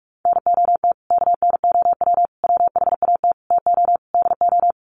Morse code, heard as a keyed tone, about 45 wpm, 700 Hz. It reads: NOT KNOW WHAT TO DO
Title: Sets of 5 Words 45wpm